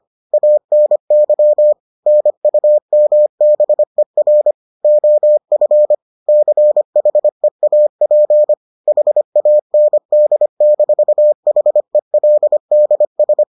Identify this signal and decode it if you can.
Morse code, heard as a keyed tone, about 25 wpm, 600 Hz. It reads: ANY NUMBER OF CHEAP HAND-HELDS